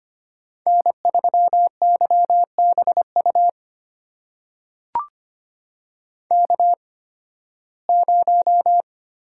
Morse code, heard as a keyed tone, about 25 wpm, spaced as for 16 wpm, 700 Hz. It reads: N3YBU K0
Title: USA Call Signs - 5 Characters - 8x Character Spacing - 25wpm